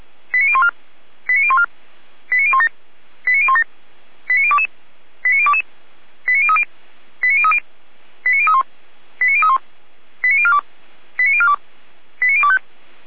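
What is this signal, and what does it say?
Five-tone selective call (ZVEI), heard as quick runs of five tones, tones of about 70 ms. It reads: 89014 89014 89017 89017 89022 89022 89030 89030 89031 89031 89042 89042 89025
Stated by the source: ZVEI five-tone selective call received on SDR, address 89011.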